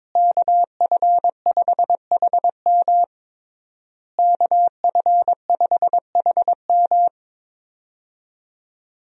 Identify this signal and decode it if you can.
Morse code, heard as a keyed tone, about 22 wpm, 700 Hz. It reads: KF5HM KF5HM
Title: USA Call Signs - 5 Characters - Repeated 3x - 22wpm